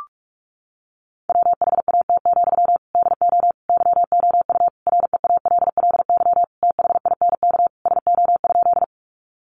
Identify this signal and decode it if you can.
Morse code, heard as a keyed tone, about 44 wpm, 700 Hz. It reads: WHAT, DO YOU REALLY THINK SO?
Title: Sentences from Top 600 Words 45wpm